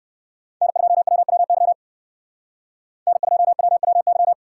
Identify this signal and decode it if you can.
Morse code, heard as a keyed tone, about 58 wpm, 700 Hz. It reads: N2WWY N2WWY